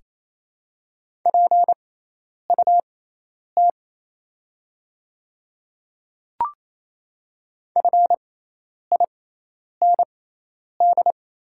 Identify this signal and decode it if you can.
Morse code, heard as a keyed tone, about 28 wpm, spaced as for 10 wpm, 700 Hz. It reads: PUT FIND
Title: 6x Character Spacing - Top 200 Words - 28wpm